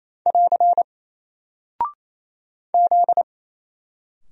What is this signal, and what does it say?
Morse code, heard as a keyed tone, about 27 wpm, 700 Hz. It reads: <AR> Z
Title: Single Character (Letter:Number:Punctuation:Prosign) Alt 28wpm